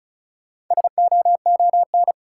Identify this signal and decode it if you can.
Morse code, heard as a keyed tone, about 36 wpm, 700 Hz. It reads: SOON
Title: Sets of 3 Words - New Word Drill - 35wpm